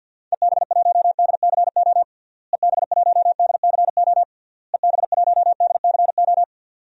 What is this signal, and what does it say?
Morse code, heard as a keyed tone, about 50 wpm, 700 Hz. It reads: EB1DCO EB1DCO EB1DCO
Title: DXCC-100 Call Signs - 6 Characters - Repeated 3x - 50wpm